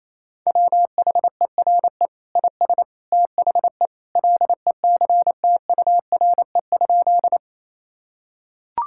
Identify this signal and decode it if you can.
Morse code, heard as a keyed tone, about 27 wpm, 700 Hz. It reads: WHERE IS THE LECTURE?
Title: Sentences from Top 1600 Words 28wpm